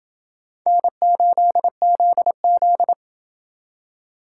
Morse code, heard as a keyed tone, about 24 wpm, 700 Hz. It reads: N8ZZ
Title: Speed Racing - USA 1x2 Call Signs 20wpm